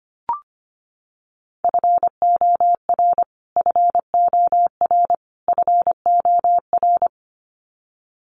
Morse code, heard as a keyed tone, about 25 wpm, 700 Hz. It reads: FOR FOR FOR
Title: Top 200 Words Repeated 25wpm